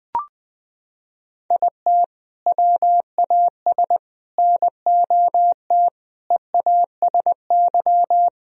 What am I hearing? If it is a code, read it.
Morse code, heard as a keyed tone, about 21 wpm, 700 Hz. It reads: IT WAS NOT EASY
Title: Sentences from Top 1000 Words - Review Part 2 - 20wpm